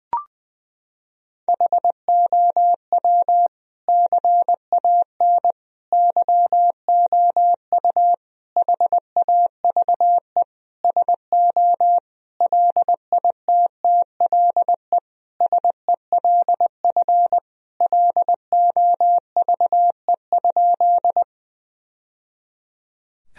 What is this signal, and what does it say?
Morse code, heard as a keyed tone, about 21 wpm, 700 Hz. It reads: HOW CAN YOU HAVE SO LITTLE SELF LOVE?
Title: Sentences from Top 1000 Words - Review Part 1 - 20wpm